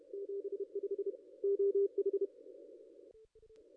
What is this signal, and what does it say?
Morse code, heard as a keyed tone, about 31 wpm, 440 Hz. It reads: 75 OH
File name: contests / ss-cw-k5tr-op-n5rz -0100